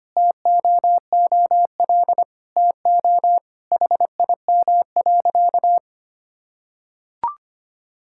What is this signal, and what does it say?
Morse code, heard as a keyed tone, about 25 wpm, 700 Hz. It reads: TOOL TO HIM.
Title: Sentences from Top 1000 Words - Review Part 6 - 25wpm